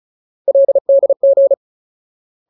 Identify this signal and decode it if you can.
Morse code, heard as a keyed tone, about 34 wpm, 550 Hz. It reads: LDG
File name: Ultimate Word List / 35WPM Ultimate